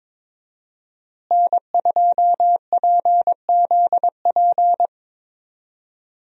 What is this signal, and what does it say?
Morse code, heard as a keyed tone, about 22 wpm, 700 Hz. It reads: N2PZP
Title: USA Call Signs - 5 Characters - Repeated 2x - 22wpm